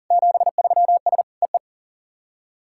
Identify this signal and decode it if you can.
Morse code, heard as a keyed tone, about 40 wpm, 700 Hz. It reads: 73S EE
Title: SOTA QSO Practice 40wpm